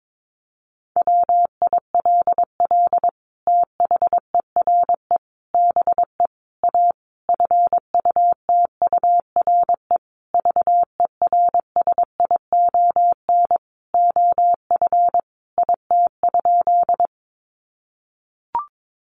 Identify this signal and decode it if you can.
Morse code, heard as a keyed tone, about 22 wpm, 700 Hz. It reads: WILL THERE BE A FUTURE VERSION OF IT?
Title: Sentences from Top 1400 Words 22wpm